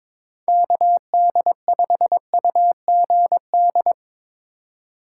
Morse code, heard as a keyed tone, about 22 wpm, 700 Hz. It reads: KD5UGD
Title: USA Call Signs - 6 Characters - 3x Character Spacing - 22wpm